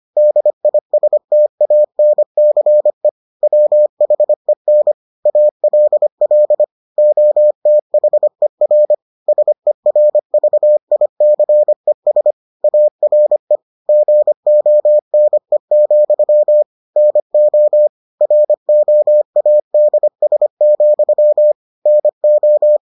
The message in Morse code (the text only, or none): DISTANCE WHEN ALL OTHER SERVICES ARE GONE, NO ROADS, NO